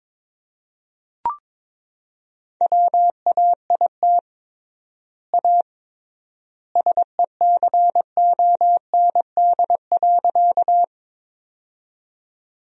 Morse code, heard as a keyed tone, about 22 wpm, 700 Hz. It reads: WAIT A SECOND.
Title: 3x Word-Spacing - Sentences from Top 500 Words - 22wpm